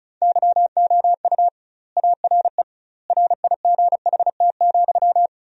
Morse code, heard as a keyed tone, about 36 wpm, 700 Hz. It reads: YOU ARE RIGHT,
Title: 2x Word-Spacing - Sentences from Top 400 Words - 35wpm